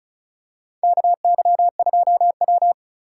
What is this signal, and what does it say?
Morse code, heard as a keyed tone, about 36 wpm, 700 Hz. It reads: KY2W
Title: USA Call Signs - 3x Character Spacing - 35wpm